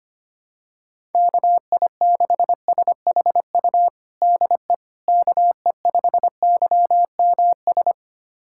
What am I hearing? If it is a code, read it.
Morse code, heard as a keyed tone, about 25 wpm, 700 Hz. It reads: KI6SHU DE KE5YMS
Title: USA Callsign DE Callsign - 6 Characters - 25wpm